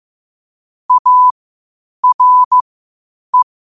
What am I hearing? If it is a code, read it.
Morse code, heard as a keyed tone, about 15 wpm, spaced as for 8 wpm, 1000 Hz. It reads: ARE
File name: qotdfiles / 8wpm